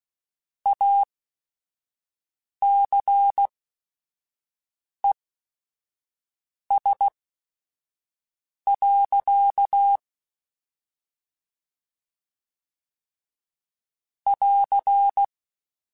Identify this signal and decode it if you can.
Morse code, heard as a keyed tone, about 16 wpm, spaced as for 5 wpm, 800 Hz. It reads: ACES. <AR>
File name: Chuckles / Chuckles-mp3s / MiddleAge @5